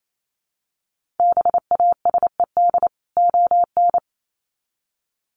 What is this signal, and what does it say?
Morse code, heard as a keyed tone, about 28 wpm, 700 Hz. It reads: BASED ON
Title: Sets of 2 Words 28wpm